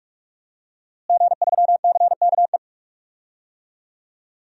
Morse code, heard as a keyed tone, about 42 wpm, 700 Hz. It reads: G3CKE
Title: DXCC-100 Call Signs - 5 Characters - Repeated 2x - 45wpm